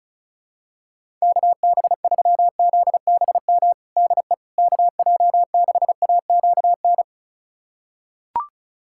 Morse code, heard as a keyed tone, about 36 wpm, 700 Hz. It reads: KB3ZBM DE KJ6AQN
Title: USA Callsign DE Callsign - 6 Characters - 35wpm